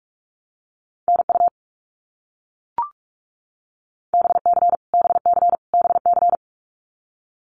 Morse code, heard as a keyed tone, about 44 wpm, 700 Hz. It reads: NU BC BC BC
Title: Canadian Provinces and Territories - Repeated 3x - 45wpm